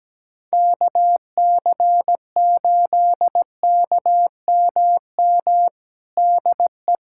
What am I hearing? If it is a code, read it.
Morse code, heard as a keyed tone, about 17 wpm, 700 Hz. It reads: KC8KMM DE
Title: USA Callsign DE Callsign 17wpm